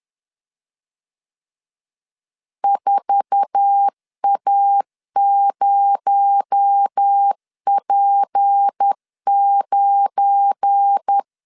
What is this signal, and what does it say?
Morse code, heard as a keyed tone, about 10 wpm, 800 Hz. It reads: 4A0P9
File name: CW / DET / 1 9 1500C S B 10wpm 4439